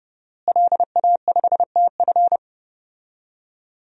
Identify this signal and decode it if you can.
Morse code, heard as a keyed tone, about 30 wpm, 700 Hz. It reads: LA5TF
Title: DXCC-100 Call Signs - 5 Characters - Repeated 2x - 30wpm